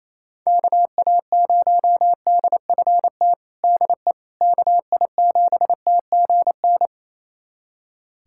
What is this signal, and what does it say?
Morse code, heard as a keyed tone, about 28 wpm, 700 Hz. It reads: KA0DFT DE KI7TGN
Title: USA Callsign DE Callsign - 6 Characters - 28wpm